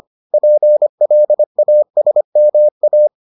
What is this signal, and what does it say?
Morse code, heard as a keyed tone, about 25 wpm, 600 Hz. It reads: PLASMA